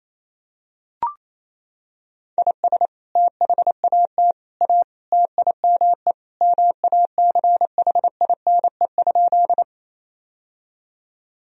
Morse code, heard as a keyed tone, about 28 wpm, 700 Hz. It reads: IS THAT A TIME MACHINE?
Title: Sentences from Top 1000 Words - Review Part 2 - 28wpm